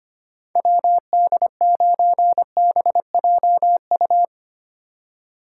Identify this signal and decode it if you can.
Morse code, heard as a keyed tone, about 25 wpm, 700 Hz. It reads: WD9BJU
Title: USA Call Signs - 6 Characters - 8x Character Spacing - 25wpm